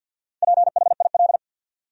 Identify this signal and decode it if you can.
Morse code, heard as a keyed tone, about 53 wpm, 700 Hz. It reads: PHIL
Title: USA Common Names - 50wpm